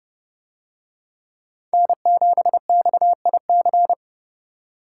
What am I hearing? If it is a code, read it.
Morse code, heard as a keyed tone, about 30 wpm, 700 Hz. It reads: N7XIC